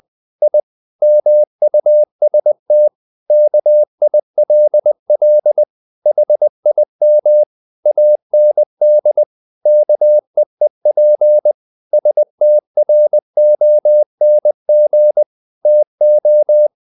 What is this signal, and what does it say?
Morse code, heard as a keyed tone, about 20 wpm, 600 Hz. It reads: I MUST KILL HIM AND KEEP STRONG TO
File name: the old man and the sea / the old man and the sea part 1 20wpm